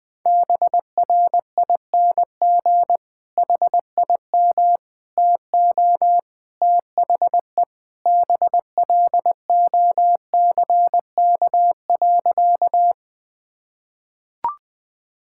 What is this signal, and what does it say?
Morse code, heard as a keyed tone, about 20 wpm, 700 Hz. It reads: BRING HIM TO THE BLOCK.
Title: Sentences from Top 1600 Words 20wpm